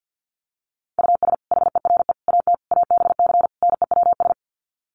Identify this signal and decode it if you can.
Morse code, heard as a keyed tone, about 50 wpm, 700 Hz. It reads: US HERE AT ABC NEWS
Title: Sets of 5 Words 50wpm